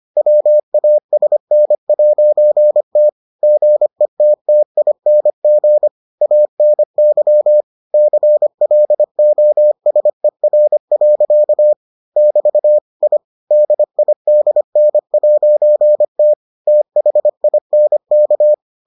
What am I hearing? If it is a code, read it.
Morse code, heard as a keyed tone, about 25 wpm, 600 Hz. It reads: WASN'T GETTING ANY CLOSER. <BT> I DIDN'T THINK